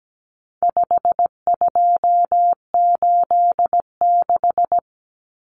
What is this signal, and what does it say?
Morse code, heard as a keyed tone, about 17 wpm, 700 Hz. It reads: 5286
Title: Numbers - 4-Digit 17wpm